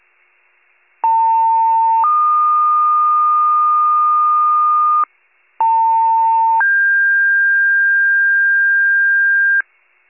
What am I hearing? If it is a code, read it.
Two-tone page, heard as a one-second tone, then a three-second tone, about 900 then 1220 Hz, about 900 then 1600 Hz, with noise. It is heard from 1.0 to 9.6 s.